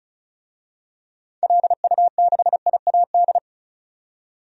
Morse code, heard as a keyed tone, about 36 wpm, 700 Hz. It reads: LU6IAD
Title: DXCC-100 Call Signs - 6 Characters - 35wpm